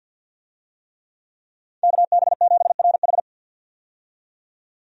Morse code, heard as a keyed tone, about 52 wpm, 700 Hz. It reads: KB7RH
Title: USA Call Signs - 5 Characters - 50wpm